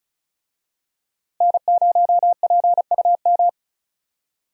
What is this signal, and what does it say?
Morse code, heard as a keyed tone, about 36 wpm, 700 Hz. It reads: N0PUM
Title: USA Call Signs - 5 Characters - Repeated 2x - 35wpm